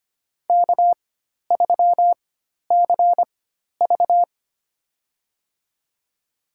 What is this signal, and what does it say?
Morse code, heard as a keyed tone, about 25 wpm, spaced as for 12 wpm, 700 Hz. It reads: K3CV
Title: USA Call Signs - 4 Characters - 4x Character Spacing - 25wpm